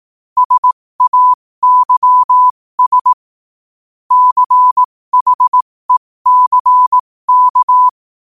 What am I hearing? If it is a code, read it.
Morse code, heard as a keyed tone, about 18 wpm, spaced as for 15 wpm, 1000 Hz. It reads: SAYS CHECK
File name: qotdfiles / 15wpm